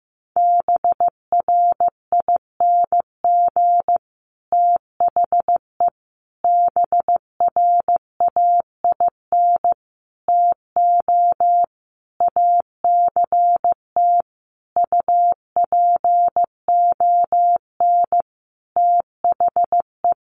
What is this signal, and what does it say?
Morse code, heard as a keyed tone, about 15 wpm, 700 Hz. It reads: BRING THE BRAIN TO ACT UPON THE
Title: Sentences from Top 1200 Words 15wpm